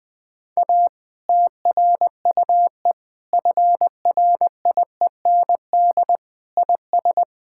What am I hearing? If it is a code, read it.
Morse code, heard as a keyed tone, about 20 wpm, 700 Hz. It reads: A TRUE FRIEND IS
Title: Difficult Sentences with Hints - 20wpm